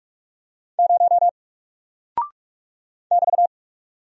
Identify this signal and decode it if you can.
Morse code, heard as a keyed tone, about 46 wpm, 700 Hz. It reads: 0 <BT>
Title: Single Character (Letter:Number:Punctuation:Prosign) Alt 45wpm